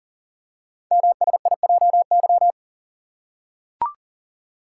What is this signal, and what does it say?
Morse code, heard as a keyed tone, about 39 wpm, 690 Hz. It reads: MSIJY
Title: Code Groups - 5 Letters - 40wpm